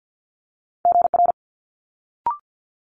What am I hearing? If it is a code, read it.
Morse code, heard as a keyed tone, about 50 wpm, 700 Hz. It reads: GR
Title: Multi-letter Phonemes Quick Fire 50wpm